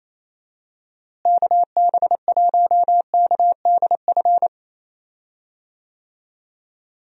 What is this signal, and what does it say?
Morse code, heard as a keyed tone, about 28 wpm, 700 Hz. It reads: KB1KDF
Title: USA Call Signs - 6 Characters - Repeated 3x - 28wpm